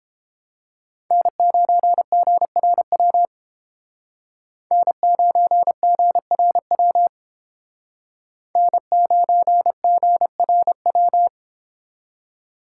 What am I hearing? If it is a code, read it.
Morse code, heard as a keyed tone, about 29 wpm, 700 Hz. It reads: N9GRW N9GRW N9GRW